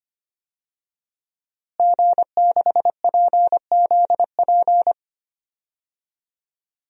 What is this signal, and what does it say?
Morse code, heard as a keyed tone, about 25 wpm, 700 Hz. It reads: G6PZP
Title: DXCC-100 Call Signs - 5 Characters - Repeated 2x - 25wpm